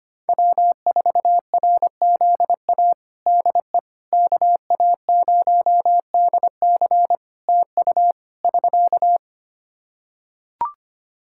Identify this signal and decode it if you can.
Morse code, heard as a keyed tone, about 25 wpm, 700 Hz. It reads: W4RZA DE KA0DC TU <SK>
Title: Practice QSO - Exchange 4 - Ending - 25wpm